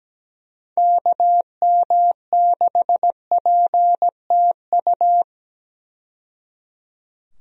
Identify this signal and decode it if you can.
Morse code, heard as a keyed tone, about 17 wpm, 700 Hz. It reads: KM6PTU
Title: USA Call Signs - 6 Characters - 17wpm